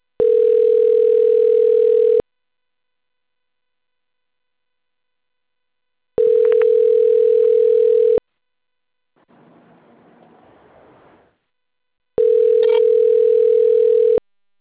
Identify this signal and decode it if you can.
Ringback tone (North America), 440 + 480 Hz.